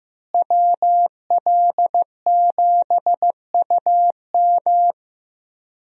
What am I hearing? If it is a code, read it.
Morse code, heard as a keyed tone, about 15 wpm, 700 Hz. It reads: WL7UM
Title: USA Call Signs - 5 Characters - Repeated 3x - 15wpm